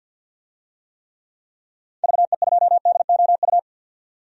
Morse code, heard as a keyed tone, about 45 wpm, 700 Hz. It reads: VE2DOU
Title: DXCC-100 Call Signs - 6 Characters - 50wpm